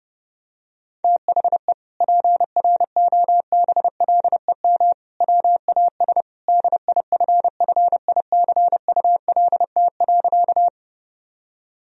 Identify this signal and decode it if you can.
Morse code, heard as a keyed tone, about 30 wpm, 700 Hz. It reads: THE PROBLEM WAS DIFFICULT.
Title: Sentences from Top 1000 Words - Review Part 2 - 30wpm